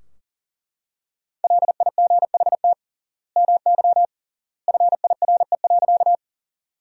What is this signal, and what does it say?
Morse code, heard as a keyed tone, about 40 wpm, 700 Hz. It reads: LIGHT MY FIRE.
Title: 3x Word-Spacing - Sentences from Top 400 Words - 40wpm